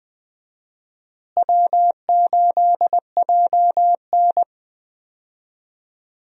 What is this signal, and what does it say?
Morse code, heard as a keyed tone, about 20 wpm, 700 Hz. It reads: W8JN
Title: USA Call Signs - 4 Characters - Repeated 2x - 20wpm